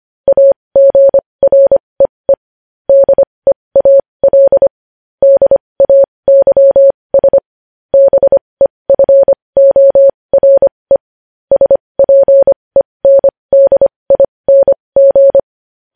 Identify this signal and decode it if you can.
Morse code, heard as a keyed tone, about 25 wpm, spaced as for 20 wpm, 550 Hz. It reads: AGREE DEAL DAYS BEFORE SPENDING
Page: News Headlines in Morse Code at 20 WPM